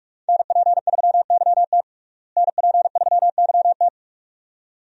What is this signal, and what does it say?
Morse code, heard as a keyed tone, about 43 wpm, 700 Hz. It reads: NP3YT NP3YT
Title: USA Call Signs - 5 Characters - Repeated 2x - 45wpm